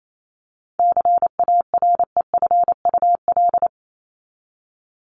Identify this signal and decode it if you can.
Morse code, heard as a keyed tone, about 28 wpm, 700 Hz. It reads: CAREFUL
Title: Top 1501-1750 Words 28wpm